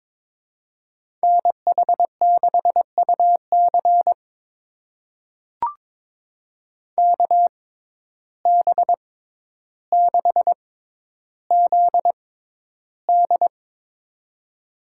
Morse code, heard as a keyed tone, about 22 wpm, spaced as for 11 wpm, 700 Hz. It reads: NH6UC KB6ZD
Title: USA Call Signs - 5 Characters - 6x Character Spacing - 22wpm